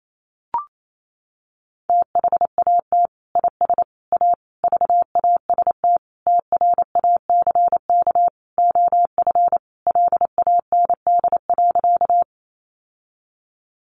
Morse code, heard as a keyed tone, about 28 wpm, 700 Hz. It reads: THAT IS A VAST TRACK OF LAND.
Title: Sentences from Top 1300 Words 28wpm